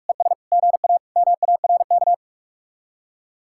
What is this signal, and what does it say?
Morse code, heard as a keyed tone, about 55 wpm, 700 Hz. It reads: ES GA MARK
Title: Speed-Racing Common QSO Phrases - 45wpm